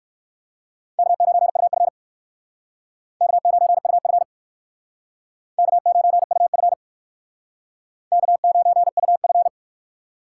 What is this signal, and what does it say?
Morse code, heard as a keyed tone, about 57 wpm, 700 Hz. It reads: K9UF K9UF K9UF K9UF